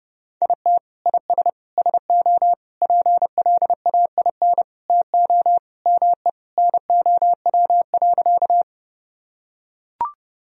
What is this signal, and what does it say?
Morse code, heard as a keyed tone, about 30 wpm, 700 Hz. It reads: IT IS SO PLAIN TO ME NOW.
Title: Sentences from Top 1000 Words - Review Part 2 - 30wpm